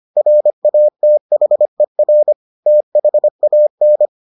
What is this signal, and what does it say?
Morse code, heard as a keyed tone, about 25 wpm, 600 Hz. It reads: RATHER THAN